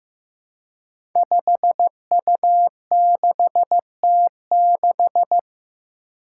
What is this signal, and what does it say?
Morse code, heard as a keyed tone, about 15 wpm, 700 Hz. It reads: 5U6T6
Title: Five Challenging Letter-Number Combinations 15wpm